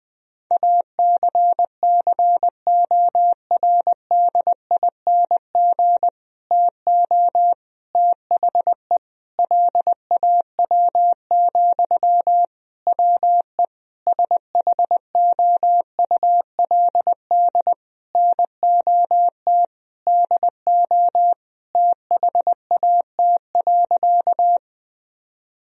Morse code, heard as a keyed tone, about 20 wpm, 700 Hz. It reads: ACCORDING TO THE LAW, WE SHOULD NOT DO THAT.
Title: Sentences from Top 1000 Words - Review Part 4 - 20wpm